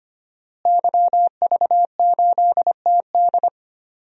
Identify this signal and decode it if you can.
Morse code, heard as a keyed tone, about 25 wpm, 700 Hz. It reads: YV8TD
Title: DXCC-100 Call Signs - 5 Characters - Repeated 2x - 25wpm